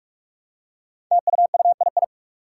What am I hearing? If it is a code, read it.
Morse code, heard as a keyed tone, about 44 wpm, 700 Hz. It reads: TUUII
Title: Five Challenging Letter-Number Combinations 45wpm